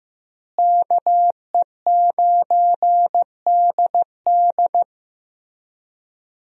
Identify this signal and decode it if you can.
Morse code, heard as a keyed tone, about 16 wpm, 700 Hz. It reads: KE9DD
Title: USA Call Signs - 5 Characters - Repeated 3x - 15wpm